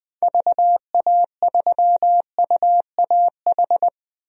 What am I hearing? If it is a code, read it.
Morse code, heard as a keyed tone, about 21 wpm, 700 Hz. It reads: VA3UAH
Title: DXCC-100 Call Signs - 6 Characters - Repeated 3x - 20wpm